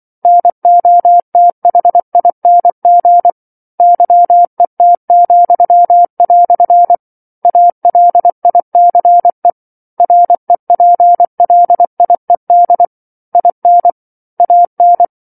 Morse code, heard as a keyed tone, about 24 wpm, 700 Hz. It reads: NOTHING YET," ALICE REPLIED IN AN
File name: 07-Chapter-07 24 WPM